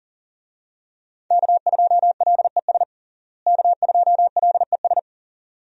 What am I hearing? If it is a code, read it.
Morse code, heard as a keyed tone, about 38 wpm, 700 Hz. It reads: K2LES K2LES
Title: USA Call Signs - 5 Characters - Repeated 2x - 40wpm